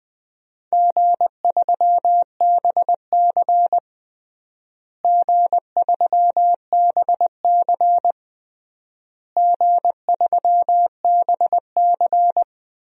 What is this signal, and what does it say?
Morse code, heard as a keyed tone, about 20 wpm, 700 Hz. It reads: G3BC G3BC G3BC
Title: DXCC-100 Call Signs - 4 Characters - Repeated 3x - 20wpm